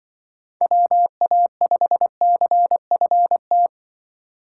Morse code, heard as a keyed tone, about 24 wpm, 700 Hz. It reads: WA5CFT
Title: Speed Racing - USA 2x3 Call Signs 20wpm